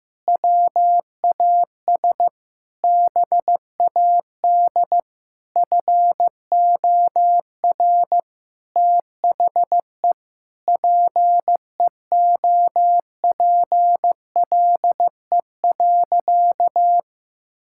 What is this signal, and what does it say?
Morse code, heard as a keyed tone, about 15 wpm, 700 Hz. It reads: WAS BAD FOR THE PEOPLE.
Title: Sentences from Top 1000 Words - Review Part 5 - 15wpm